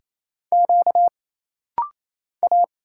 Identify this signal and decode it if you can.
Morse code, heard as a keyed tone, about 27 wpm, 700 Hz. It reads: Q A
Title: Single Letters - Rapid-Fire 28wpm